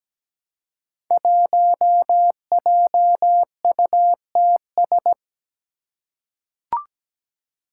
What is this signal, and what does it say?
Morse code, heard as a keyed tone, about 17 wpm, 700 Hz. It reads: 1JUTS
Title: Five Challenging Letter-Number Combinations 17wpm